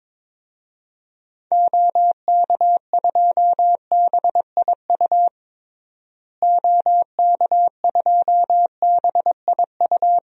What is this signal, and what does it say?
Morse code, heard as a keyed tone, about 23 wpm, 700 Hz. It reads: OK2BIU OK2BIU
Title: DXCC-100 Call Signs - 6 Characters - Repeated 2x - 22wpm